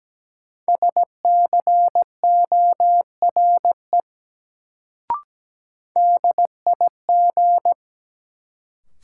Morse code, heard as a keyed tone, about 17 wpm, 700 Hz. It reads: SCORE DIG
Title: Sentences from Top 1600 Words - New Word Drill Alt - 17wpm